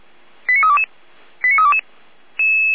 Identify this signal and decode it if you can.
Five-tone selective call (ZVEI), heard as quick runs of five tones, tones of about 70 ms. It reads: 89322 89322